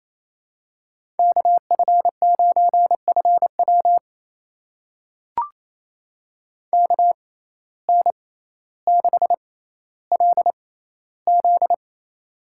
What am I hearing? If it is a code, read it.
Morse code, heard as a keyed tone, about 28 wpm, spaced as for 14 wpm, 700 Hz. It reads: KF9FW KN6LZ
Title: USA Call Signs - 5 Characters - 6x Character Spacing - 28wpm